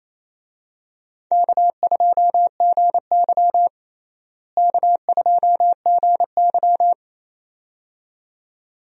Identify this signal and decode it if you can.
Morse code, heard as a keyed tone, about 28 wpm, 700 Hz. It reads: K2GY K2GY
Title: USA Call Signs - 4 Characters - Repeated 2x - 28wpm